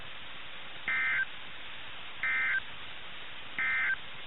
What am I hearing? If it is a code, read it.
Emergency Alert System (SAME) tones, heard as harsh buzzing bursts. It is heard from 0.8 to 4.0 s.
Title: NOAA Weather Radio - KIH61